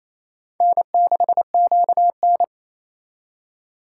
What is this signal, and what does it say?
Morse code, heard as a keyed tone, about 34 wpm, 700 Hz. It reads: N6QN
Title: Speed Racing - USA 1x2 Call Signs 28wpm